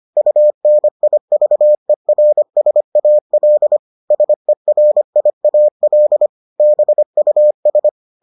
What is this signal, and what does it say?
Morse code, heard as a keyed tone, about 25 wpm, 600 Hz. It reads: UNIVERSAL SERIAL BUS